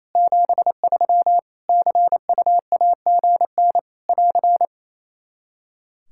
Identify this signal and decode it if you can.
Morse code, heard as a keyed tone, about 28 wpm, 700 Hz. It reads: 73 CUAGN <AR>
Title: Practice QSO - Exchange 4 - Ending - 28wpm